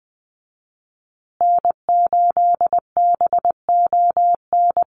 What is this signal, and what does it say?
Morse code, heard as a keyed tone, about 20 wpm, 700 Hz. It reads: N8BON
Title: USA Call Signs Repeated 2x 20wpm